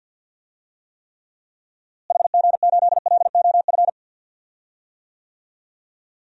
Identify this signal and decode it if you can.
Morse code, heard as a keyed tone, about 52 wpm, 700 Hz. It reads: HG8LOF
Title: DXCC-100 Call Signs - 6 Characters - Repeated 2x - 50wpm